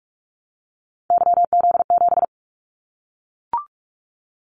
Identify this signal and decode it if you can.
Morse code, heard as a keyed tone, about 44 wpm, 700 Hz. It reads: YZ7
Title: Callsign Prefixes for Top 25 DXCC - Repeated 3x - 45wpm